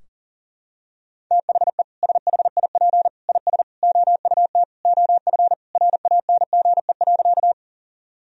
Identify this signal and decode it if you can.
Morse code, heard as a keyed tone, about 37 wpm, 700 Hz. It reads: THE SHIP IS OUT OF RANGE.
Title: Sentences from Top 1000 Words - Review Part 4 - 40wpm